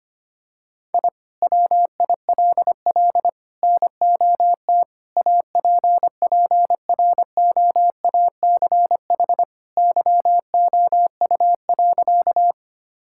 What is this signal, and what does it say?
Morse code, heard as a keyed tone, about 25 wpm, 700 Hz. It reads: I WILL NOT APPROACH YOU.
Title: Sentences from Top 1000 Words - Review Part 6 - 25wpm